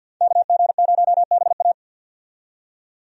Morse code, heard as a keyed tone, about 52 wpm, 700 Hz. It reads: KG0BA
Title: USA Call Signs - 5 Characters - 50wpm